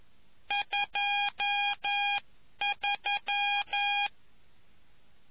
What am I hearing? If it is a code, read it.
Morse code, heard as a keyed tone, about 11 wpm, 2450 Hz, with noise. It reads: 23